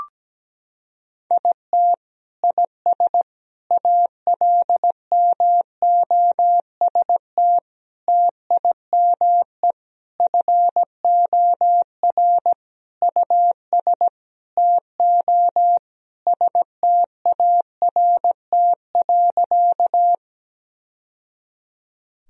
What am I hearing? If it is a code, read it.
Morse code, heard as a keyed tone, about 17 wpm, 700 Hz. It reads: IT IS ALMOST TIME FOR US TO START.
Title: Sentences from Top 1000 Words - Review Part 4 - 17wpm